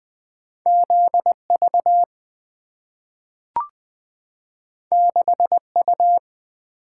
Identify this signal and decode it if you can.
Morse code, heard as a keyed tone, about 20 wpm, 700 Hz. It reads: ZV 6U
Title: Two Letter-Number Combinations - 20wpm